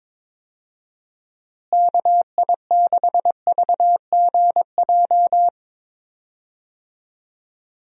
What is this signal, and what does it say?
Morse code, heard as a keyed tone, about 21 wpm, 700 Hz. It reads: KI6VGJ
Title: USA Call Signs - 6 Characters - Repeated 2x - 22wpm